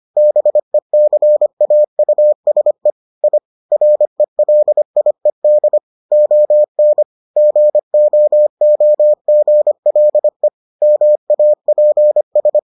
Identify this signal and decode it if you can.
Morse code, heard as a keyed tone, about 25 wpm, 600 Hz. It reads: BECAUSE I RELIED ON GOOGLE MAPS